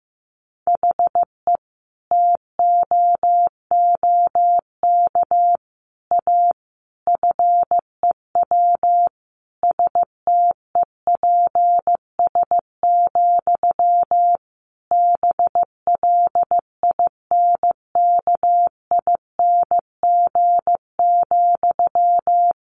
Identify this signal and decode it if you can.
Morse code, heard as a keyed tone, about 15 wpm, 700 Hz. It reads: HE TOOK A FEW STEPS, BLINKING,
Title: A Sound Of Thunder 15wpm